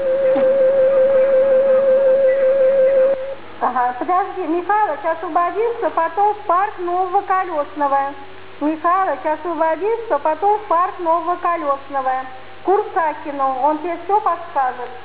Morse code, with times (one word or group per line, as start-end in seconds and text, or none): none